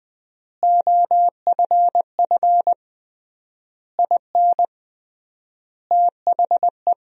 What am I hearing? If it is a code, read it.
Morse code, heard as a keyed tone, about 20 wpm, 700 Hz. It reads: OFF IN THE
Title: 3x Word-Spacing - Sentences from Top 200 Words - 20wpm